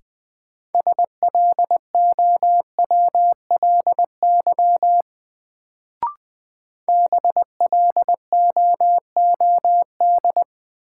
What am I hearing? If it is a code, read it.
Morse code, heard as a keyed tone, about 20 wpm, 700 Hz. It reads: SLOWLY BLOOD
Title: Sentences from Top 1100 Words - New Word Drill Alt - 20wpm